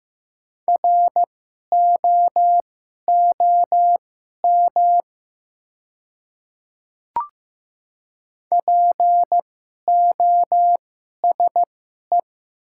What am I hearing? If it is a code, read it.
Morse code, heard as a keyed tone, about 15 wpm, spaced as for 11 wpm, 700 Hz. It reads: ROOM POSE
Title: 2x Character Spacing - Top 500 Words - 15wpm